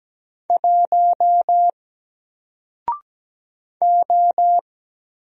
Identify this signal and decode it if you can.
Morse code, heard as a keyed tone, about 17 wpm, 700 Hz. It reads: 1 O
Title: Single Letter-Number Alt 17wpm